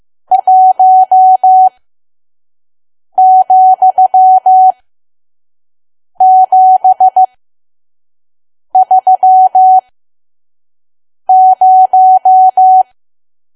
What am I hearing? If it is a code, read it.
Morse code, heard as a keyed tone, about 15 wpm, spaced as for 5 wpm, 750 Hz. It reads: 1,730
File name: Archive / 5 WPM files / PM